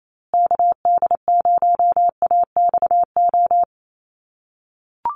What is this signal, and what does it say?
Morse code, heard as a keyed tone, about 28 wpm, 700 Hz. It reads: KD0AXO
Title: USA Call Signs Repeated 2x 28wpm